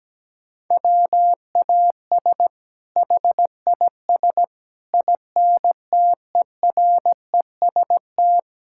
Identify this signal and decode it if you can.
Morse code, heard as a keyed tone, about 17 wpm, 700 Hz. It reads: WAS HIS INTEREST
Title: Sentences from Top 1000 Words - Review Part 5 - 17wpm